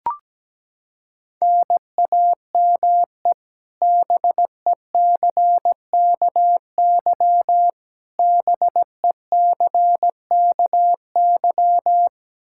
Morse code, heard as a keyed tone, about 17 wpm, 700 Hz. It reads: NAME BECKY BECKY
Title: K1USN SST QSO Practice 17wpm